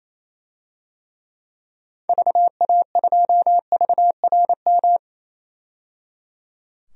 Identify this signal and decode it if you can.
Morse code, heard as a keyed tone, about 28 wpm, 700 Hz. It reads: VA2VRM
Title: DXCC-100 Call Signs - 6 Characters - 28wpm